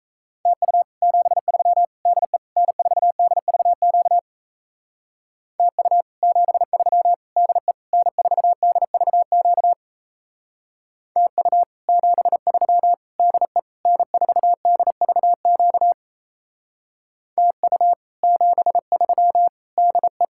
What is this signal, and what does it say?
Morse code, heard as a keyed tone, about 34 wpm, 700 Hz. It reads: TU 73 DE N4DVQ TU 73 DE N4DVQ TU 73 DE N4DVQ TU 73 DE
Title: Speed-Racing Common QSO Phrases Encore - 28wpm